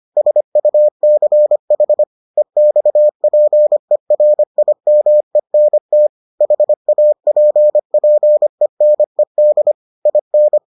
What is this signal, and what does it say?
Morse code, heard as a keyed tone, about 25 wpm, 600 Hz. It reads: SUCH EXPERIMENT HAPPENED IN